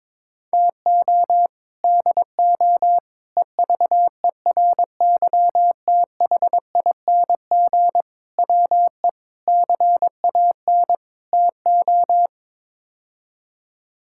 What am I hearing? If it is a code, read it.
Morse code, heard as a keyed tone, about 22 wpm, 700 Hz. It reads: TO DO EVERYTHING WE CAN TO
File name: practice / 22wpm / Sets of 6 Words - 22wpm